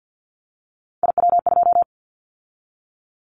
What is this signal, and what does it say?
Morse code, heard as a keyed tone, about 50 wpm, 700 Hz. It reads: IW2
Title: Callsign Prefixes for Top 50 DXCC - 50wpm